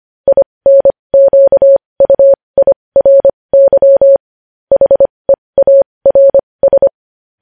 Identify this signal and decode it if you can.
Morse code, heard as a keyed tone, about 25 wpm, spaced as for 20 wpm, 550 Hz. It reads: INQUIRY HEARS